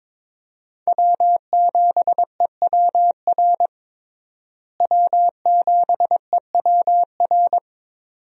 Morse code, heard as a keyed tone, about 22 wpm, 700 Hz. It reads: W7EWR W7EWR
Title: USA Call Signs - 5 Characters - Repeated 2x - 22wpm